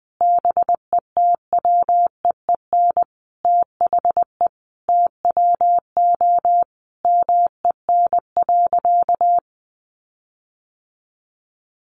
Morse code, heard as a keyed tone, about 20 wpm, 700 Hz. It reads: BETWEEN THE TWO MEN.
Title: Sentences from Top 1300 Words 20wpm